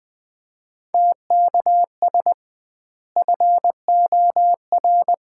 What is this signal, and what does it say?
Morse code, heard as a keyed tone, about 20 wpm, 700 Hz. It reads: TKS FOR
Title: Common QSO Phrases - 2x Word-Spacing - 20wpm